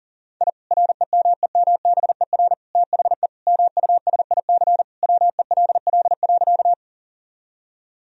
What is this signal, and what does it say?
Morse code, heard as a keyed tone, about 40 wpm, 700 Hz. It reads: I REMEMBER THE MUSIC WELL.
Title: Sentences from Top 1000 Words - Review Part 1 - 40wpm